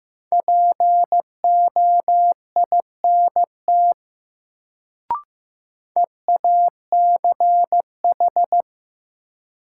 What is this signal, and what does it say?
Morse code, heard as a keyed tone, about 16 wpm, 700 Hz. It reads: POINT EACH
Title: Sentences from Top 200 Words - New Word Drill - 15wpm